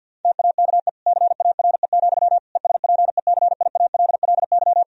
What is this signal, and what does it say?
Morse code, heard as a keyed tone, about 47 wpm, 700 Hz. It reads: TAKE CARE, ESPECIALLY
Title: Sentences from Top 1000 Words - Review Part 4 - 50wpm